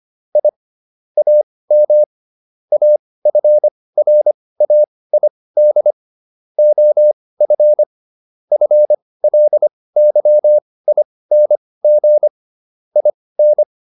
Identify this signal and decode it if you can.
Morse code, heard as a keyed tone, about 25 wpm, spaced as for 18 wpm, 600 Hz. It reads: I AM AFRAID OF FLYING IN